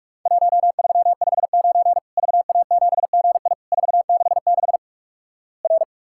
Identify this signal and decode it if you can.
Morse code, heard as a keyed tone, about 48 wpm, 700 Hz. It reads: 1359 VA7GI 466 R
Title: Worked All Europe DX Contest - QTC Exchange - 45wpm